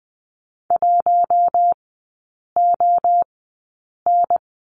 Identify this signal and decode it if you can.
Morse code, heard as a keyed tone, about 20 wpm, spaced as for 8 wpm, 700 Hz. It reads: 1ON
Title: 041 Lesson 8e - Number 1 - Three Characters 20wpm